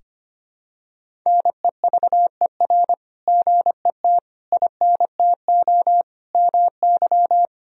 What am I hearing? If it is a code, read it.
Morse code, heard as a keyed tone, about 25 wpm, 700 Hz. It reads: NEVER GET INTO MY
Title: Sentences from Top 1000 Words - Review Part 3 - 25wpm